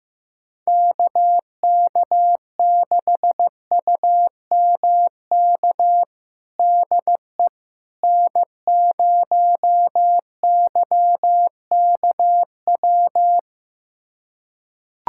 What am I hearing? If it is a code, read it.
Morse code, heard as a keyed tone, about 15 wpm, 700 Hz. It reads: KK6UMK DE N0YKW
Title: USA Callsign DE Callsign 15wpm